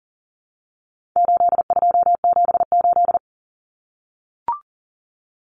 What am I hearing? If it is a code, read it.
Morse code, heard as a keyed tone, about 40 wpm, 700 Hz. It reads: 8278
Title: Numbers - 4-Digit 40wpm